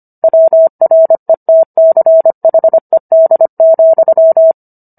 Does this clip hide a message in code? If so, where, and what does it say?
Yes: Morse code from 0.2 to 4.5 s, heard as a keyed tone, about 25 wpm, 650 Hz. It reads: WRETCHED,